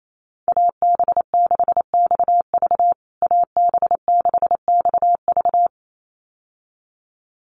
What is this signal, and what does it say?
Morse code, heard as a keyed tone, about 28 wpm, 700 Hz. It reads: AB6XV AB6XV
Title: USA Call Signs Repeated 2x 28wpm